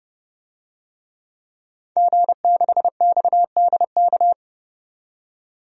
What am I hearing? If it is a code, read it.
Morse code, heard as a keyed tone, about 30 wpm, 700 Hz. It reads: G6XDK
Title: DXCC-100 Call Signs - 5 Characters - 30wpm